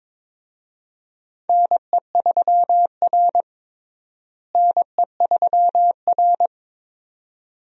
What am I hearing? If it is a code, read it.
Morse code, heard as a keyed tone, about 22 wpm, 700 Hz. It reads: NE3R NE3R
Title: USA Call Signs - 4 Characters - Repeated 2x - 22wpm